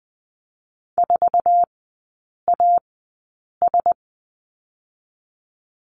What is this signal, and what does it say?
Morse code, heard as a keyed tone, about 20 wpm, 700 Hz. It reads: 4 A S
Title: 107 Lesson 18e - Letter m - Three Characters 20wpm